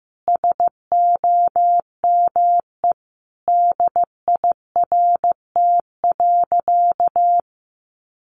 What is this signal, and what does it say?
Morse code, heard as a keyed tone, about 15 wpm, 700 Hz. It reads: SOME DIRT.
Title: A Sound Of Thunder 15wpm